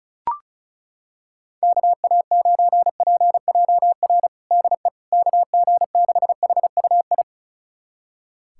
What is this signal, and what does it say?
Morse code, heard as a keyed tone, about 36 wpm, 700 Hz. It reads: KA9PJR DE KG6HUI
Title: USA Callsign DE Callsign 35wpm